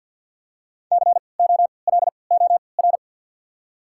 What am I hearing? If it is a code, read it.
Morse code, heard as a keyed tone, about 52 wpm, spaced as for 29 wpm, 690 Hz. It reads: COLOR
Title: 3x Character Spacing - Top 300 Words - 50wpm